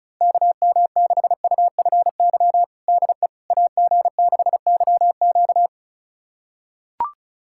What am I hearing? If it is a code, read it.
Morse code, heard as a keyed tone, about 36 wpm, 700 Hz. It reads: KM6UFY DE AG6YQ
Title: USA Callsign DE Callsign 35wpm